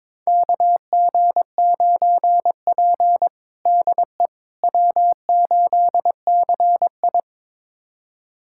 Morse code, heard as a keyed tone, about 22 wpm, 700 Hz. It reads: KG9P DE W8CI
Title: USA Callsign DE Callsign - 4 Characters - 22wpm